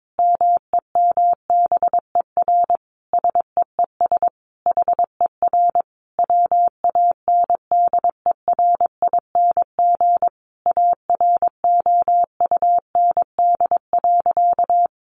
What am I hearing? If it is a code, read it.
Morse code, heard as a keyed tone, about 22 wpm, 700 Hz. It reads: MEMBER SEES HER WANDERING AROUND.
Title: Additional Easy Phrases 22wpm